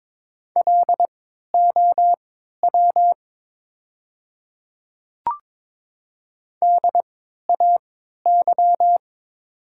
Morse code, heard as a keyed tone, about 22 wpm, spaced as for 12 wpm, 700 Hz. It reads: LOW DAY
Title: 3x Character Spacing - Top 200 Words - 22wpm